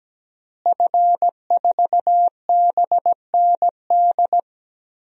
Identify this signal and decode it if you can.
Morse code, heard as a keyed tone, about 17 wpm, 700 Hz. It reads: F4BND
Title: DXCC-100 Call Signs - 5 Characters - Repeated 3x - 17wpm